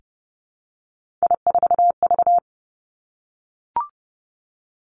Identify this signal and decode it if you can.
Morse code, heard as a keyed tone, about 30 wpm, 700 Hz. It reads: I4V
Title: Three Challenging Letter-Number Combinations 30wpm